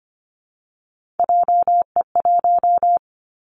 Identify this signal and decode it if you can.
Morse code, heard as a keyed tone, about 25 wpm, 700 Hz. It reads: JE1
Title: Callsign Prefixes for Top 25 DXCC - Repeated 3x - 25wpm